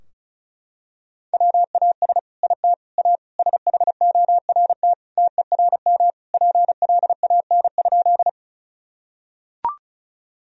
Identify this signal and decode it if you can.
Morse code, heard as a keyed tone, about 36 wpm, 700 Hz. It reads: WAS IT A SHORT TERM PLAN?
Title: Sentences from Top 1000 Words - Review Part 4 - 35wpm